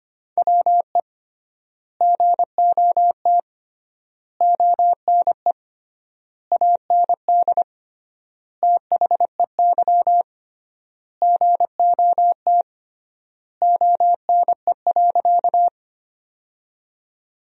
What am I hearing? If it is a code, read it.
Morse code, heard as a keyed tone, about 25 wpm, 700 Hz. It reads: WE GOT ONE AND THEY GOT ONE.
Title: 3x Word-Spacing - Sentences from Top 300 Words - 25wpm